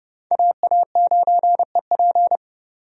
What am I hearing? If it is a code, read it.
Morse code, heard as a keyed tone, about 31 wpm, 700 Hz. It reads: AA9EP
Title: USA Call Signs - 5 Characters - 30wpm